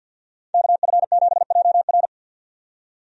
Morse code, heard as a keyed tone, about 46 wpm, 700 Hz. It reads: KF7JR
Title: USA Call Signs - 5 Characters - 50wpm